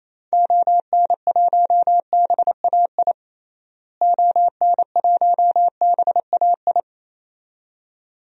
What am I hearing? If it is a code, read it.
Morse code, heard as a keyed tone, about 28 wpm, 700 Hz. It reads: ON1BAI ON1BAI
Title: DXCC-100 Call Signs - 6 Characters - Repeated 2x - 28wpm